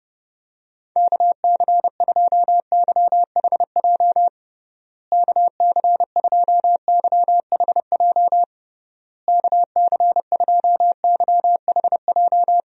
Morse code, heard as a keyed tone, about 30 wpm, 700 Hz. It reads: KC2YHJ KC2YHJ KC2YHJ
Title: USA Call Signs - 6 Characters - Repeated 3x - 30wpm